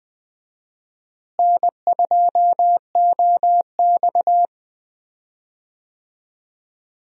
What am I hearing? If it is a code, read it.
Morse code, heard as a keyed tone, about 20 wpm, 700 Hz. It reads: N2OX
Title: USA Call Signs - 4 Characters - Repeated 3x - 20wpm